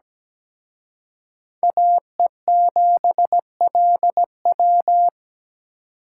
Morse code, heard as a keyed tone, about 20 wpm, 700 Hz. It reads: AE7LW
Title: Speed Racing - USA 2x2 Call Signs 17wpm